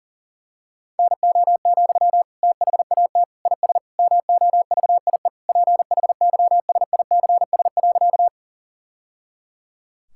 Morse code, heard as a keyed tone, about 40 wpm, 700 Hz. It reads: NO, THAT IS MOVIE PHYSICS.
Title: Sentences from Top 1000 Words - Review Part 5 - 40wpm